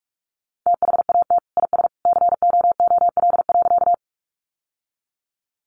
Morse code, heard as a keyed tone, about 44 wpm, 700 Hz. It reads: THAT IS COOL.
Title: Sentences from Top 1200 Words 45wpm